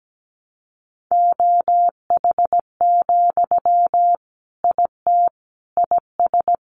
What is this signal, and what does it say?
Morse code, heard as a keyed tone, about 17 wpm, 700 Hz. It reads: OH, IT IS
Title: Sentences from Top 1200 Words 17wpm